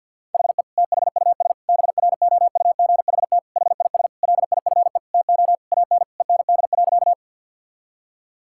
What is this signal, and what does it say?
Morse code, heard as a keyed tone, about 52 wpm, 700 Hz. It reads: HE THUS BROUGHT HIS LIFE TO AN END.
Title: Sentences from Top 1000 Words - Review Part 5 - 50wpm